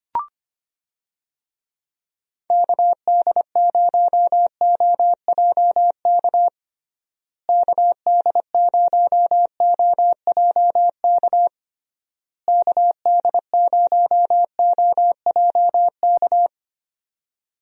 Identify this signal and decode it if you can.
Morse code, heard as a keyed tone, about 25 wpm, 700 Hz. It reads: KD0OJK KD0OJK KD0OJK
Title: USA Call Signs - 6 Characters - Repeated 3x - 25wpm